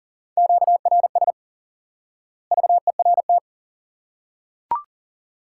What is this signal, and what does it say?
Morse code, heard as a keyed tone, about 40 wpm, 700 Hz. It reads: QRS VERT
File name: IQE Course 3 - Lesson 89 - YAGI - Two Elements - 40wpm